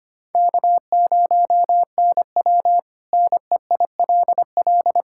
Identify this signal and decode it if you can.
Morse code, heard as a keyed tone, about 25 wpm, 700 Hz. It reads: K0NW NEILL
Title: K1USN SST QSO Practice - 25wpm